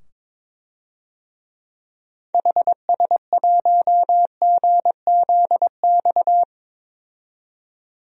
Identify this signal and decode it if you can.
Morse code, heard as a keyed tone, about 22 wpm, 700 Hz. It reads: HS1GZX
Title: DXCC-100 Call Signs - 6 Characters - 22wpm